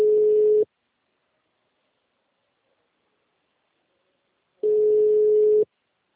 Ringback tone (Europe, 425 Hz).
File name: out- -2034- - - .53512